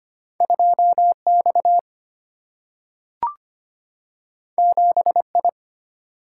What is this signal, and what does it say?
Morse code, heard as a keyed tone, about 25 wpm, 700 Hz. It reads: 2X 7I
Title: Single-Letter Single-Number 25wpm